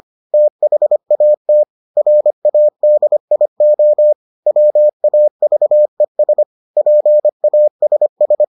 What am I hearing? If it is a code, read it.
Morse code, heard as a keyed tone, about 25 wpm, 600 Hz. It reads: THAT RADIO WAVES PASS